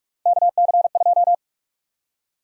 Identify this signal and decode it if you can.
Morse code, heard as a keyed tone, about 43 wpm, 700 Hz. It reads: KC2
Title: USA Call Sign Prefixes - 45wpm